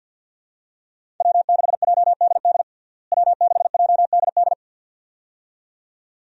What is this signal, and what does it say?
Morse code, heard as a keyed tone, about 47 wpm, 700 Hz. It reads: W6JDD W6JDD
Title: USA Call Signs - 5 Characters - Repeated 2x - 50wpm